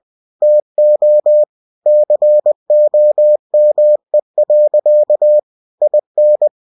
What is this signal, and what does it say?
Morse code, heard as a keyed tone, about 20 wpm, 600 Hz. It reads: TO COME. IN
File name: cw-books / winnie-the-pooh / winnie-20wpm-page-0120